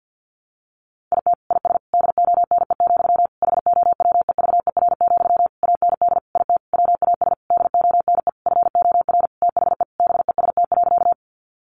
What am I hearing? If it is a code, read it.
Morse code, heard as a keyed tone, about 50 wpm, 700 Hz. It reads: IT IS DONE, HOWEVER, AND IT WAS DONE FOR THE BEST.
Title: Sentences from Top 600 Words 50wpm